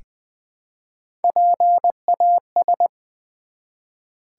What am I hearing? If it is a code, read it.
Morse code, heard as a keyed tone, about 20 wpm, 700 Hz. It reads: PAS
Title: Build A Word - 1 Letter At A Time - 4-Letter Words - 20wpm